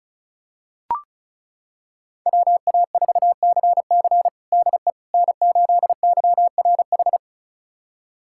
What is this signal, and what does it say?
Morse code, heard as a keyed tone, about 34 wpm, 700 Hz. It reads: WA4CC DE N8YRH
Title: USA Callsign DE Callsign - 5 Characters - 35wpm